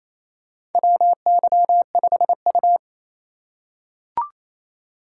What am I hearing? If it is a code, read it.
Morse code, heard as a keyed tone, about 28 wpm, 700 Hz. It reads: WY5U
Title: USA Call Signs - 2x Character Spacing - 28wpm